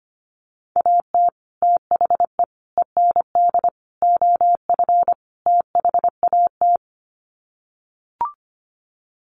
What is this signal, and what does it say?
Morse code, heard as a keyed tone, about 25 wpm, 700 Hz. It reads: AT THE END OF THAT
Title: Sets of 5 Words Encore2 25wpm